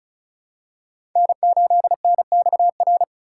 Morse code, heard as a keyed tone, about 36 wpm, 700 Hz. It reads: N8NXR
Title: USA Call Signs - 5 Characters - 2x Character Spacing - 35wpm